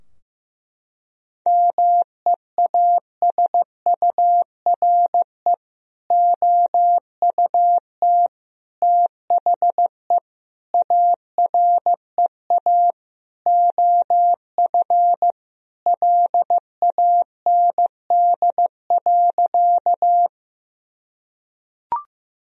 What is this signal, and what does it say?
Morse code, heard as a keyed tone, about 16 wpm, 700 Hz. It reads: MEASURE OUT THE AREA OF LAND.
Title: Sentences from Top 1000 Words - Review Part 2 - 15wpm